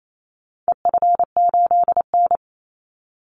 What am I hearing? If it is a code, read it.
Morse code, heard as a keyed tone, about 28 wpm, 700 Hz. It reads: EF8N
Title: Call Signs Repeated 3x - 28wpm